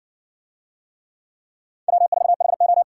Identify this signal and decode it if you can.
Morse code, heard as a keyed tone, about 70 wpm, 700 Hz. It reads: W4HO